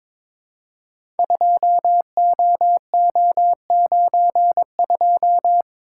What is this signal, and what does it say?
Morse code, heard as a keyed tone, about 23 wpm, 700 Hz. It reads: 2OO92
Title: Code Groups - 5 Letter:Numbers - 22wpm